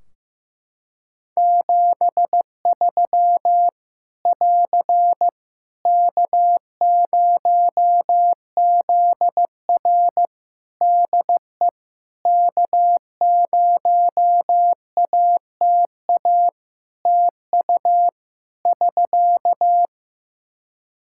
Morse code, heard as a keyed tone, about 15 wpm, 700 Hz. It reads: 73 <AR> K0ZR DE K0ATA TU <SK>
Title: Practice QSO - Exchange 4 - Ending - 15wpm